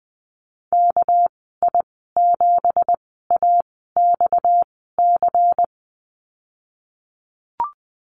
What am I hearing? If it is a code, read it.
Morse code, heard as a keyed tone, about 20 wpm, spaced as for 15 wpm, 700 Hz. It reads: KI7AXC
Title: 231 Lesson 37g - Letter x - Call Signs 20wpm